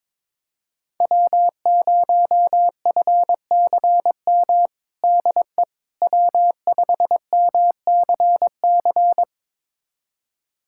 Morse code, heard as a keyed tone, about 22 wpm, 700 Hz. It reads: W0FCM DE W5MCC
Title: USA Callsign DE Callsign - 5 Characters - 22wpm